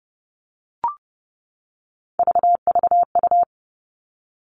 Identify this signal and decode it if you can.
Morse code, heard as a keyed tone, about 30 wpm, 700 Hz. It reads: VVU
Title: Three Challenging Letter-Number Combinations 30wpm